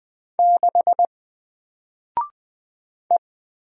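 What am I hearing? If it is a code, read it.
Morse code, heard as a keyed tone, about 20 wpm, 700 Hz. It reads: 6 E
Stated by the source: Single letter and number Morse drill at 20 wpm, from N4LCD.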